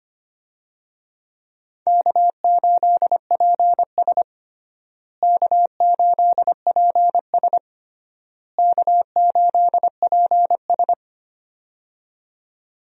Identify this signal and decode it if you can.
Morse code, heard as a keyed tone, about 25 wpm, 700 Hz. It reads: K8PS K8PS K8PS
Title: USA Call Signs - 4 Characters - Repeated 3x - 25wpm